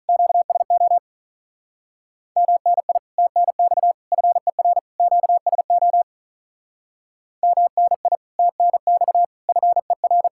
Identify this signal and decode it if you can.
Morse code, heard as a keyed tone, about 46 wpm, 700 Hz. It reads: QSO MNI TNX FER QSO MNI TNX FER